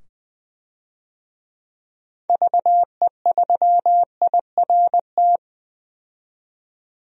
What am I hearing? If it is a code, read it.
Morse code, heard as a keyed tone, about 20 wpm, 700 Hz. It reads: VE3IRT
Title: DXCC-100 Call Signs - 6 Characters - 20wpm